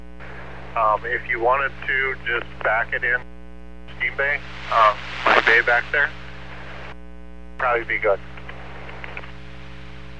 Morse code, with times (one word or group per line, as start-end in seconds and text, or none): none